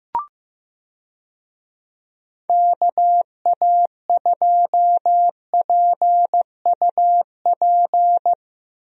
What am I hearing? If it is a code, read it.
Morse code, heard as a keyed tone, about 15 wpm, 700 Hz. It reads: KA2PUP
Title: USA Call Signs - 6 Characters - Repeated 3x - 15wpm